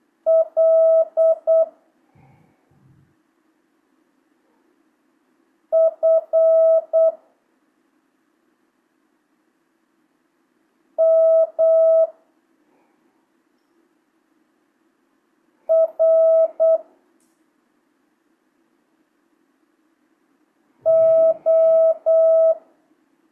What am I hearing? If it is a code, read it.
Morse code, heard as a keyed tone, about 8 wpm, 650 Hz. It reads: L F M R O
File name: jh8cbh / 2025cw